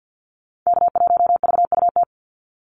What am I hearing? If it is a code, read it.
Morse code, heard as a keyed tone, about 50 wpm, 700 Hz. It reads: K1VUT
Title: Call Signs - 50wpm